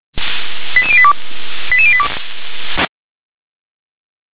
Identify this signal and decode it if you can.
Five-tone selective call (ZVEI), heard as quick runs of five tones, tones of about 80 ms. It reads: 88082 88082
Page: /357/SDR/ZVEI/